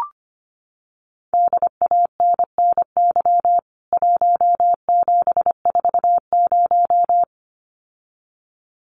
Morse code, heard as a keyed tone, about 25 wpm, 700 Hz. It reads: DANNY 1740
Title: CWT Exchange 25wpm